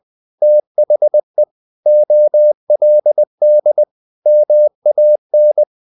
Morse code, heard as a keyed tone, about 20 wpm, 600 Hz. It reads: THE OLD MAN
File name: the old man and the sea / the old man and the sea part 1 20wpm